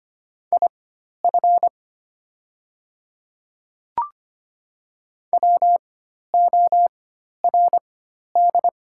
Morse code, heard as a keyed tone, about 25 wpm, spaced as for 12 wpm, 700 Hz. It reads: IF WORD
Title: 4x Character Spacing - Top 200 Words - 25wpm